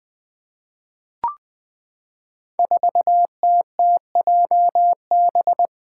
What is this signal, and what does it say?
Morse code, heard as a keyed tone, about 20 wpm, 700 Hz. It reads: 4TTJB
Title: Five Challenging Letter-Number Combinations 20wpm